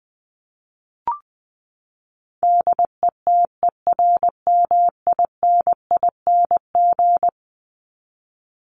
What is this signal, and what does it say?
Morse code, heard as a keyed tone, about 20 wpm, 700 Hz. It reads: DETERMINING
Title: ING Suffix - Top 100 Words with Base Words 20wpm